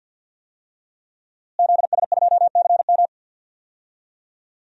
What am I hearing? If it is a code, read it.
Morse code, heard as a keyed tone, about 52 wpm, 700 Hz. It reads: ZS2CM
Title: DXCC-100 Call Signs - 5 Characters - Repeated 2x - 50wpm